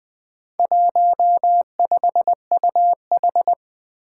Morse code, heard as a keyed tone, about 20 wpm, 700 Hz. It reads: 15UH
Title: Four Challenging Letter-Number Combinations 20wpm